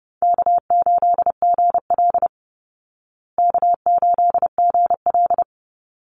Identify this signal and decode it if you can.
Morse code, heard as a keyed tone, about 30 wpm, 700 Hz. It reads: K8GL K8GL
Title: Easy Call Signs Repeated 2x - 30wpm